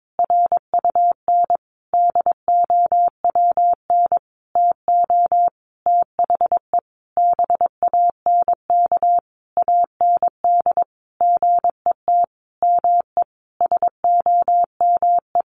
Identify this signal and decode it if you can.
Morse code, heard as a keyed tone, about 22 wpm, 700 Hz. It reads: RUN DOWN TO THE BANK AND GET ME SOME
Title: Sentences from Top 1500 Words 22wpm